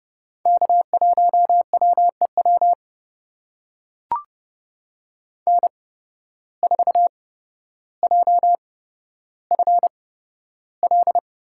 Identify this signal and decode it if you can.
Morse code, heard as a keyed tone, about 30 wpm, spaced as for 13 wpm, 700 Hz. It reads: K1WEW N4JFL
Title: USA Call Signs - 5 Characters - 8x Character Spacing - 30wpm